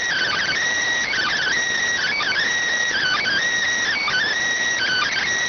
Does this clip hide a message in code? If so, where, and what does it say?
Yes: Emergency Alert System (SAME) tones from 1.1 to 5.3 s, heard as harsh buzzing bursts.